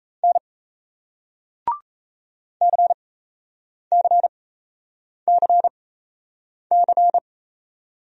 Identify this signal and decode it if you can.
Morse code, heard as a keyed tone, about 38 wpm, spaced as for 8 wpm, 700 Hz. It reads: N CCCC